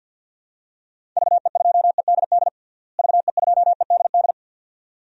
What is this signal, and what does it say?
Morse code, heard as a keyed tone, about 46 wpm, 700 Hz. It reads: VE2EDD VE2EDD
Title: DXCC-100 Call Signs - 6 Characters - Repeated 2x - 50wpm